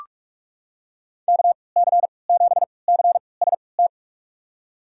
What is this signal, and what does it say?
Morse code, heard as a keyed tone, about 42 wpm, spaced as for 25 wpm, 700 Hz. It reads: KC7CST
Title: USA Call Signs - 6 Characters - 3x Character Spacing - 45wpm